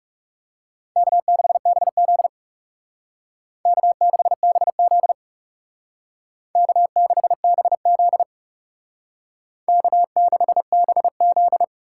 Morse code, heard as a keyed tone, about 36 wpm, 700 Hz. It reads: K6BZ K6BZ K6BZ K6BZ
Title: Speed Racing - USA 1x2 Call Signs 30wpm